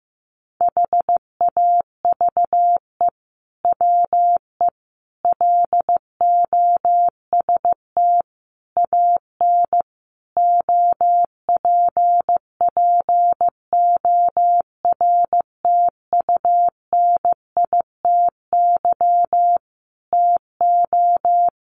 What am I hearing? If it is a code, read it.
Morse code, heard as a keyed tone, about 15 wpm, 700 Hz. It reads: HAVE WE LOST AN OPPORTUNITY TO
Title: Sentences from Top 1400 Words 15wpm